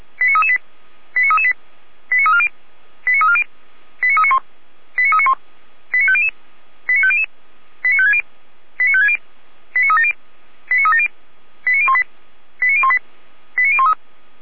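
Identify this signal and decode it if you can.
Five-tone selective call (ZVEI), heard as quick runs of five tones, tones of about 70 ms. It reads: 89338 89338 89340 89340 89391 89391 89500 89500 89560 89560 89370 89370 89018 89018 89013
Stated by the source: ZVEI five-tone call 89333 received by SDR.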